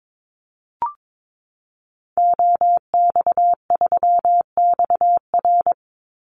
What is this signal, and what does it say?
Morse code, heard as a keyed tone, about 22 wpm, 700 Hz. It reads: OX3XR
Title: Call Signs Repeated 3x - 22wpm